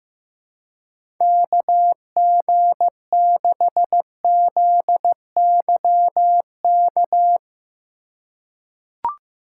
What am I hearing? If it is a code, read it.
Morse code, heard as a keyed tone, about 15 wpm, 700 Hz. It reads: KG6ZYK
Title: USA Call Signs - 6 Characters - 2x Character Spacing - 15wpm